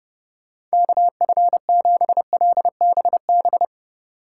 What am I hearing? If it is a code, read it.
Morse code, heard as a keyed tone, about 29 wpm, 700 Hz. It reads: KF7LBB